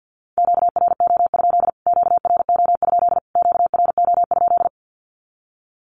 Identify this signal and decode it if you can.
Morse code, heard as a keyed tone, about 50 wpm, 700 Hz. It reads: QRO? QRO? QRO?
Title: Q-Codes Repeated 3x 50wpm